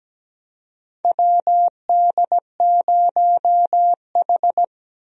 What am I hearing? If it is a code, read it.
Morse code, heard as a keyed tone, about 17 wpm, 700 Hz. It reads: WD0H
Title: USA Call Signs - 4 Characters - 17wpm